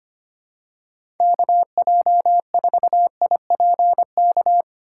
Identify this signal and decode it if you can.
Morse code, heard as a keyed tone, about 25 wpm, 700 Hz. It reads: KJ4IPK
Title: USA Call Signs - 6 Characters - 2x Character Spacing - 25wpm